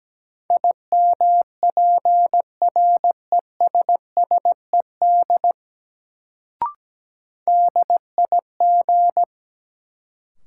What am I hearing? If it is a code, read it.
Morse code, heard as a keyed tone, about 17 wpm, 700 Hz. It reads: IMPRESSED DIG
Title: Sentences from Top 1600 Words - New Word Drill Alt - 17wpm